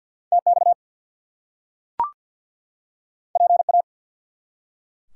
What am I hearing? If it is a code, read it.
Morse code, heard as a keyed tone, about 52 wpm, 700 Hz. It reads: TX PA
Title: US States and Canadian Provinces 50wpm